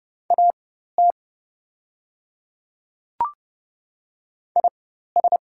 Morse code, heard as a keyed tone, about 29 wpm, spaced as for 14 wpm, 700 Hz. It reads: AT IS
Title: 4x Character Spacing - Top 100 Words - 30wpm